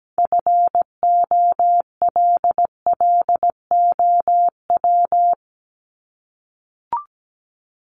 Morse code, heard as a keyed tone, about 17 wpm, 700 Hz. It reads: FOLLOW
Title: ING Suffix - Top 100 Words with Base Words 17wpm